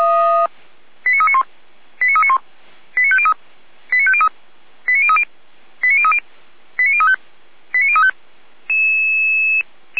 Five-tone selective call (ZVEI), heard as quick runs of five tones, tones of about 70 ms. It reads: 89391 89391 89503 89503 89030 89030 89035 89035